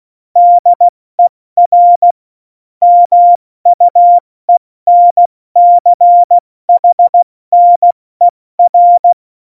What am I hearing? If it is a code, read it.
Morse code, heard as a keyed tone, about 16 wpm, 700 Hz. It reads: DER MUENCHNER
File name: Morsekurs / Witze / Witz 10.Dezember 2019, 70 F80